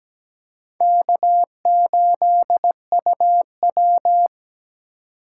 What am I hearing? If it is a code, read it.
Morse code, heard as a keyed tone, about 17 wpm, 700 Hz. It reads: K8UW
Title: USA Call Signs - 3x Character Spacing - 17wpm